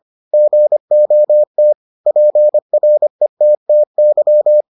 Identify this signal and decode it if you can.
Morse code, heard as a keyed tone, about 25 wpm, 600 Hz. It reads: GOT PRETTY